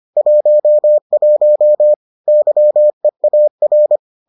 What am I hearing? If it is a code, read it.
Morse code, heard as a keyed tone, about 25 wpm, 600 Hz. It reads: 11 YEAR